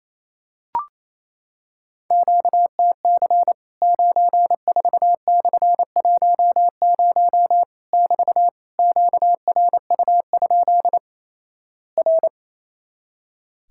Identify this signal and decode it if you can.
Morse code, heard as a keyed tone, about 28 wpm, 700 Hz. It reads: QTC 94/10 <BT> QRU? R
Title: Worked All Europe DX Contest - QTC Exchange - 28wpm